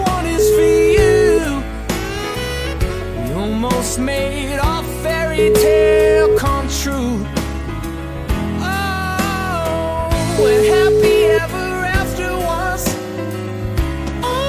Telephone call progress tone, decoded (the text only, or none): ringback tone